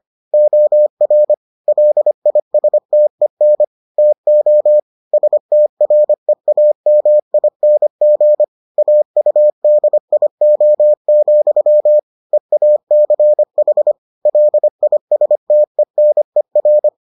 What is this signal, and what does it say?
Morse code, heard as a keyed tone, about 25 wpm, 600 Hz. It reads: OR LISTEN TO STREAMING AUDIO, EACH LISTENER